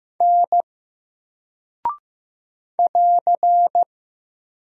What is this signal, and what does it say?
Morse code, heard as a keyed tone, about 15 wpm, 700 Hz. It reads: N <AR>
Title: Single Character (Letter:Number:Punctuation:Prosign) Alt 15wpm